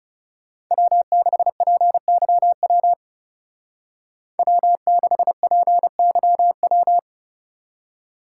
Morse code, heard as a keyed tone, about 35 wpm, 700 Hz. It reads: W6PYW W6PYW
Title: Speed Racing - USA 1x3 Call Signs 30wpm